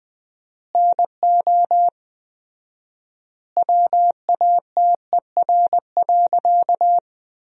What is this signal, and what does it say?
Morse code, heard as a keyed tone, about 20 wpm, 700 Hz. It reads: NO WATER.
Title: 4x Word-Spacing - Sentences from Top 100 Words - 20wpm